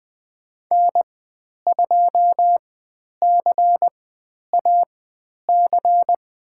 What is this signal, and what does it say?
Morse code, heard as a keyed tone, about 20 wpm, spaced as for 10 wpm, 700 Hz. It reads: N2CAC
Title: 109 Lesson 18g - Letter m - Call Signs 20wpm